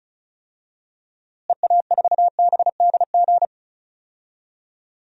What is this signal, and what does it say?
Morse code, heard as a keyed tone, about 36 wpm, 700 Hz. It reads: EA4BDG
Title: DXCC-100 Call Signs - 6 Characters - Repeated 3x - 35wpm